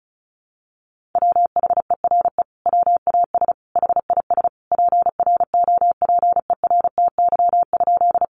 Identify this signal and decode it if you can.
Morse code, heard as a keyed tone, about 34 wpm, 700 Hz. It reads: WHERE WAS HIS PROPERTY?
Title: Sentences from Top 1000 Words 35wpm